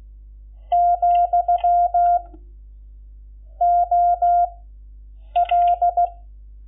Morse code, heard as a keyed tone, about 16 wpm, spaced as for 7 wpm, 700 Hz. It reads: , OL
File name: Lecciones / CLASE Nº11